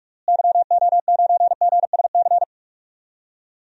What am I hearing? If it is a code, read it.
Morse code, heard as a keyed tone, about 46 wpm, 700 Hz. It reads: YO9GSC
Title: DXCC-100 Call Signs - 6 Characters - Repeated 3x - 45wpm